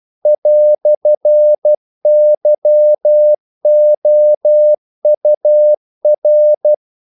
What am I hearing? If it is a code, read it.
Morse code, heard as a keyed tone, about 12 wpm, 600 Hz. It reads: "YOUR
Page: English quotations in CW (telegraphy - morse)